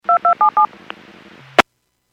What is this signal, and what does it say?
DTMF, heard as telephone touch tones, tones of about 110 ms. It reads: 2277
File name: Businesses / 00271 15-13 Businesses 451.25 WPS